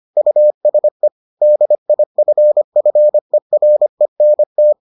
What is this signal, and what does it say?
Morse code, heard as a keyed tone, about 25 wpm, 600 Hz. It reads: USE DIFFERENT